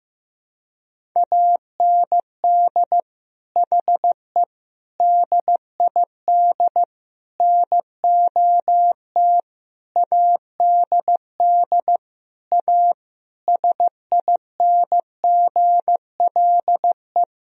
Morse code, heard as a keyed tone, about 15 wpm, 700 Hz. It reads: AND HE DID NOT ADD A SINGLE
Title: Sentences from Top 1000 Words - Review Part 1 - 15wpm